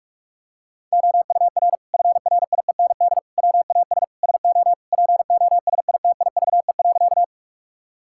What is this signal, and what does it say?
Morse code, heard as a keyed tone, about 46 wpm, 700 Hz. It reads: OUR FRIEND WAS SO POSITIVE.
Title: Sentences from Top 1000 Words - Review Part 5 - 45wpm